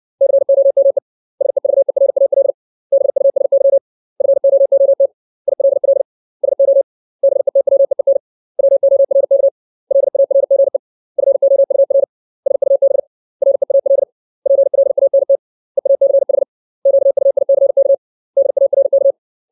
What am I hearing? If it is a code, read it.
Morse code, heard as a keyed tone, about 60 wpm, 550 Hz. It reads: COME SEVERAL BUSY FOOT ILL SO BETWEEN WORK LARGE FOUR SUB REAL PLANT EACH QUICK DARK
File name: 632 words / 632 Words 60WPM